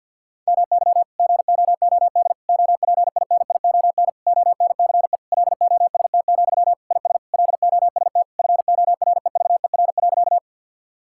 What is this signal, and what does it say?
Morse code, heard as a keyed tone, about 50 wpm, 700 Hz. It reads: MY GOOD OPINION ONCE LOST, IS LOST FOREVER.
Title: Sentences from Top 1600 Words 50wpm